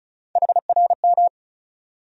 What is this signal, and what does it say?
Morse code, heard as a keyed tone, about 34 wpm, 700 Hz. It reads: HRM
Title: USA Call Sign Postfixes - 3 Letters 35wpm